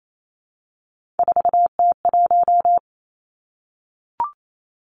Morse code, heard as a keyed tone, about 28 wpm, 700 Hz. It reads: 4T1
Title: Three Challenging Letter-Number Combinations 28wpm